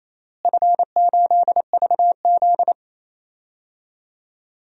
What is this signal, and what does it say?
Morse code, heard as a keyed tone, about 27 wpm, 700 Hz. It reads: F8VZ
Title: DXCC-100 Call Signs - 4 Characters - Repeated 3x - 28wpm